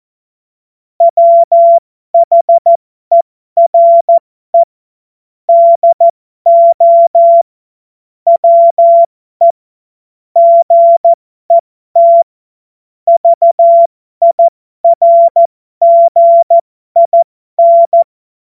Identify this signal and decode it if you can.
Morse code, heard as a keyed tone, about 14 wpm, spaced as for 12 wpm, 680 Hz. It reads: WHERE DO WE GET VIRGIN
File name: Morsekurs / Witze